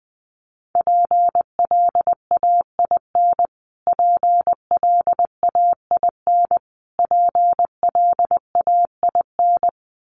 Morse code, heard as a keyed tone, about 20 wpm, 700 Hz. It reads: PLAIN PLAIN PLAIN
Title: Top 500 Words Repeated 20wpm